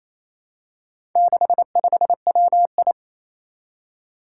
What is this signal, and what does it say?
Morse code, heard as a keyed tone, about 28 wpm, 700 Hz. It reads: 65WI
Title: Four Challenging Letter-Number Combinations 28wpm